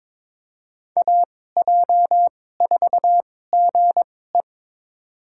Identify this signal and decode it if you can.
Morse code, heard as a keyed tone, about 22 wpm, spaced as for 16 wpm, 700 Hz. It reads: AJ4GE
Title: USA Call Signs - 5 Characters - 2x Character Spacing - 22wpm